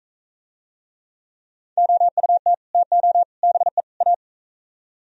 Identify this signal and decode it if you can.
Morse code, heard as a keyed tone, about 42 wpm, 700 Hz. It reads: OUT TO BE A